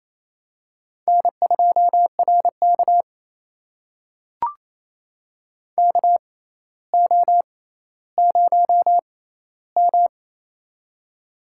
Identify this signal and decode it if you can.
Morse code, heard as a keyed tone, about 28 wpm, spaced as for 14 wpm, 700 Hz. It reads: N2RK KO0M
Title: USA Call Signs - 4 Characters - 6x Character Spacing - 28wpm